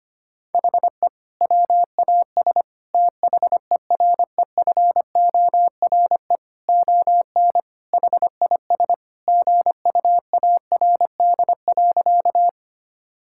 Morse code, heard as a keyed tone, about 25 wpm, 700 Hz. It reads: HE WAS THEREFORE ON HIS GUARD.
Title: Sentences from Top 1000 Words - Review Part 5 - 25wpm